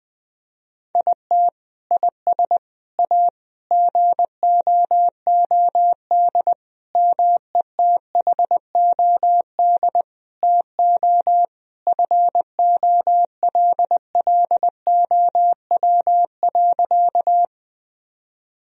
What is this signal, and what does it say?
Morse code, heard as a keyed tone, about 20 wpm, 700 Hz. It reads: IT IS A GOOD METHOD TO FOLLOW.
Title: Sentences from Top 1000 Words - Review Part 3 - 20wpm